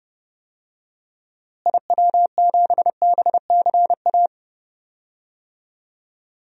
Morse code, heard as a keyed tone, about 30 wpm, 700 Hz. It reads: IW7BCA
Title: DXCC-100 Call Signs - 6 Characters - Repeated 2x - 30wpm